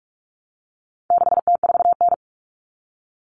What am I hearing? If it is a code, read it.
Morse code, heard as a keyed tone, about 44 wpm, 700 Hz. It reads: 6T4N
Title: Common Cut Numbers - 4-Digit 45wpm